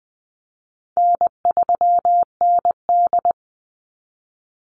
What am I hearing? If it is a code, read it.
Morse code, heard as a keyed tone, about 20 wpm, 700 Hz. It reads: N3ND
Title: Easy Call Signs Repeated 3x - 20wpm